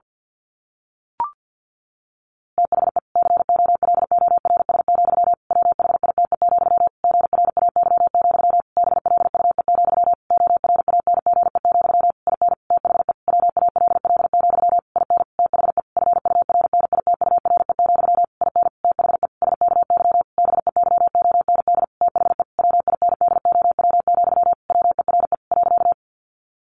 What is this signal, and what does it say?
Morse code, heard as a keyed tone, about 50 wpm, 700 Hz. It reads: THE COLORS, WHITE, GRAY, BLUE, ORANGE, IN THE WALL, IN THE FURNITURE, IN THE SKY BEYOND THE WINDOW, WERE .
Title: A Sound Of Thunder 50wpm